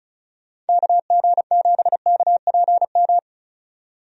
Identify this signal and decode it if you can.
Morse code, heard as a keyed tone, about 36 wpm, 700 Hz. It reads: KG7KPM
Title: USA Call Signs - 6 Characters - 35wpm